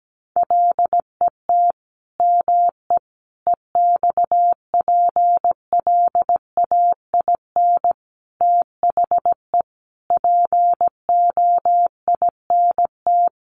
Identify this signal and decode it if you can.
Morse code, heard as a keyed tone, about 17 wpm, 700 Hz. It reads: LET ME EXPLAIN THE POINT
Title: Sentences from Top 1100 Words 17wpm